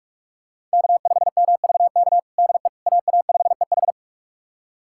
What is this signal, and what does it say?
Morse code, heard as a keyed tone, about 46 wpm, 700 Hz. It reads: K5MVK DE AA5EH
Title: USA Callsign DE Callsign - 5 Characters - 45wpm